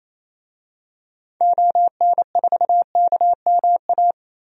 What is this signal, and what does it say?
Morse code, heard as a keyed tone, about 28 wpm, 700 Hz. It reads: ON4KMA
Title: DXCC-100 Call Signs - 6 Characters - 28wpm